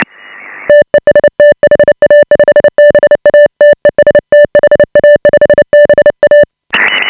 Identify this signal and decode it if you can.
Morse code, heard as a keyed tone, about 30 wpm, 590 Hz. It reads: TEST HA5BA TEST HA5BA
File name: AUDIO2018 / HA5BA CW